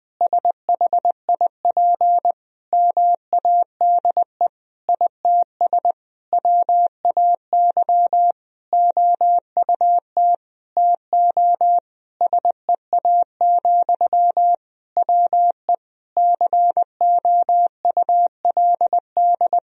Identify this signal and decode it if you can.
Morse code, heard as a keyed tone, about 21 wpm, 700 Hz. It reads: SHIP MADE ITS WAY OUT TO SEA, WE COULD
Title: Sentences from Top 1000 Words - Review Part 2 - 20wpm